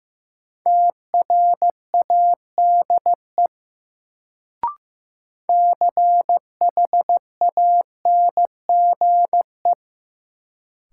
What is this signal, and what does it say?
Morse code, heard as a keyed tone, about 15 wpm, 700 Hz. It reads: TRADE CHANGE
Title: Sentences from Top 600 Words - New Word Drill - 15wpm